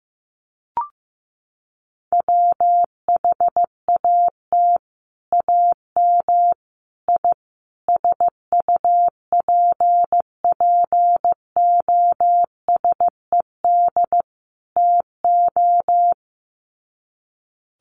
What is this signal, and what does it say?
Morse code, heard as a keyed tone, about 15 wpm, 700 Hz. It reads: WHAT AM I SUPPOSED TO
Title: Sets of 5 Words Encore2 15wpm